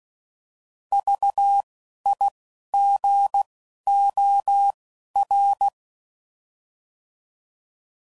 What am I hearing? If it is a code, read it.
Morse code, heard as a keyed tone, about 16 wpm, 800 Hz. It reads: V I G O R
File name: Prefixes / By First Letter / Vs